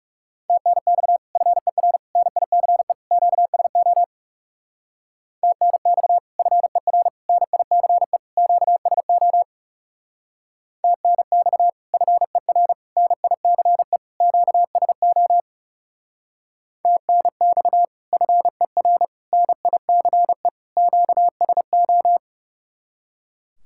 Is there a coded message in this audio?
Yes: Morse code from 0.5 to 22.2 s, heard as a keyed tone, about 39 wpm, 700 Hz. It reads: TNX FER NICE QSO TNX FER NICE QSO TNX FER NICE QSO TNX FER NICE QSO